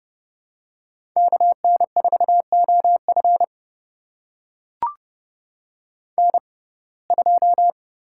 Morse code, heard as a keyed tone, about 30 wpm, spaced as for 22 wpm, 700 Hz. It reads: KN4OF N2
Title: USA Call Signs - 5 Characters - 6x Character Spacing - 30wpm